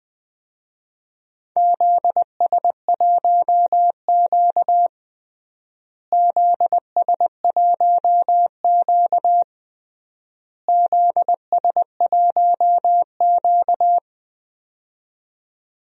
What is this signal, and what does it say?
Morse code, heard as a keyed tone, about 20 wpm, 700 Hz. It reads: ZS1Q ZS1Q ZS1Q
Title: DXCC-100 Call Signs - 4 Characters - Repeated 3x - 20wpm